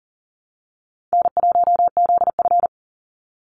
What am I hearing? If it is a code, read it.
Morse code, heard as a keyed tone, about 38 wpm, 700 Hz. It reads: N1ZF
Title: NA SOTA Activator Callsigns 40wpm